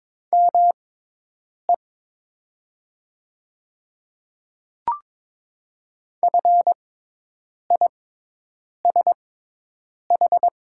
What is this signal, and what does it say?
Morse code, heard as a keyed tone, about 22 wpm, spaced as for 8 wpm, 700 Hz. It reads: ME FISH
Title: 6x Character Spacing - Top 300 Words - 22wpm